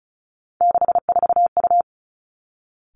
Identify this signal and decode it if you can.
Morse code, heard as a keyed tone, about 36 wpm, 700 Hz. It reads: 64U
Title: Three Challenging Letter-Number Combinations 35wpm